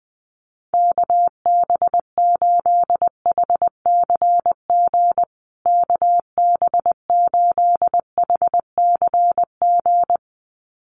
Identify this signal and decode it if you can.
Morse code, heard as a keyed tone, about 20 wpm, 700 Hz. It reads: KB8HCG KB8HCG
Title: USA Call Signs Repeated 2x 20wpm